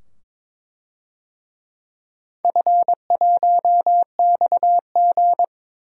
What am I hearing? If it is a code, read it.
Morse code, heard as a keyed tone, about 22 wpm, 690 Hz. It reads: F1XG
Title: DXCC-100 Call Signs - 4 Characters - 22wpm